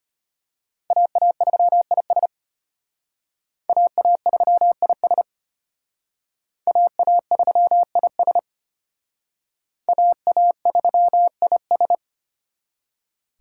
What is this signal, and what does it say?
Morse code, heard as a keyed tone, about 32 wpm, 700 Hz. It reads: AA3IS AA3IS AA3IS AA3IS